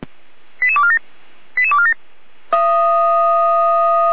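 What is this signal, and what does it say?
Five-tone selective call (ZVEI), heard as quick runs of five tones, tones of about 70 ms. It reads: 88247 88247
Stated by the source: ZVEI five-tone selective call received on an SDR.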